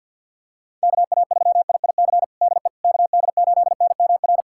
Morse code, heard as a keyed tone, about 50 wpm, 690 Hz. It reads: KA3IIC DE KD8NMR
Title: USA Callsign DE Callsign - 6 Characters - 50wpm